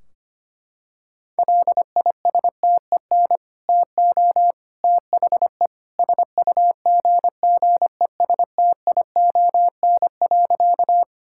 Morse code, heard as a keyed tone, about 25 wpm, 700 Hz. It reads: LISTEN TO THE SUGGESTION.
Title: Sentences from Top 1600 Words 25wpm